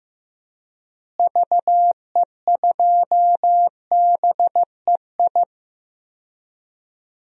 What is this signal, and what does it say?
Morse code, heard as a keyed tone, about 15 wpm, 700 Hz. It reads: VE2BEI
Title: DXCC-100 Call Signs - 6 Characters - Repeated 3x - 15wpm